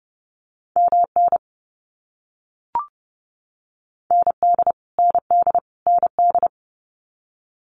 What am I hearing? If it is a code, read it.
Morse code, heard as a keyed tone, about 30 wpm, 700 Hz. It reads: MN ND ND ND
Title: US States Repeated 3x 30wpm